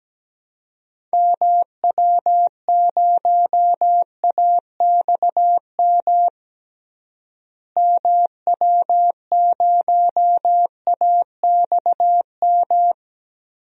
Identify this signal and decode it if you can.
Morse code, heard as a keyed tone, about 17 wpm, 700 Hz. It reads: MW0AXM MW0AXM
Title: DXCC-100 Call Signs - 6 Characters - Repeated 2x - 17wpm